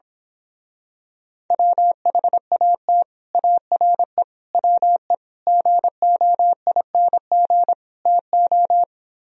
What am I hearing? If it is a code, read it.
Morse code, heard as a keyed tone, about 26 wpm, 700 Hz. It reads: WHAT ARE WE GOING TO
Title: Speed-Racing - Sets of 5 Words 22wpm